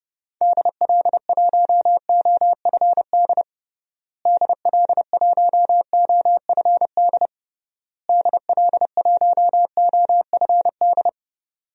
Morse code, heard as a keyed tone, about 32 wpm, 700 Hz. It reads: DL1OFD DL1OFD DL1OFD
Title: DXCC-100 Call Signs - 6 Characters - Repeated 3x - 30wpm